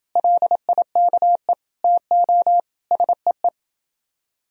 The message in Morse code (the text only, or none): LIKE TO SEE